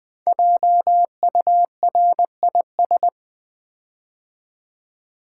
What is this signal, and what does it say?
Morse code, heard as a keyed tone, about 20 wpm, 700 Hz. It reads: JURIS
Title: Build A Word - 5 Letters At A Time - 12-Letter Words - 20wpm